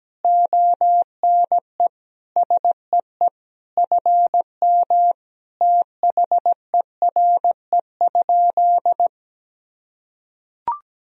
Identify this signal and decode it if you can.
Morse code, heard as a keyed tone, about 17 wpm, 700 Hz. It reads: ONE SEE FM THERE?
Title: Sentences from Top 100 Words and Abbreviations 17wpm